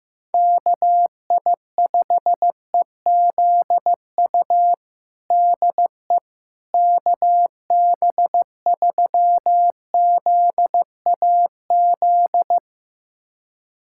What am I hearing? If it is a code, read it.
Morse code, heard as a keyed tone, about 15 wpm, 700 Hz. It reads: KI5EZU DE KB3ZAZ
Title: USA Callsign DE Callsign - 6 Characters - 15wpm